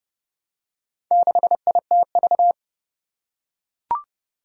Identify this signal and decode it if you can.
Morse code, heard as a keyed tone, about 31 wpm, 700 Hz. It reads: 6ITV
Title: Four Challenging Letter-Number Combinations 30wpm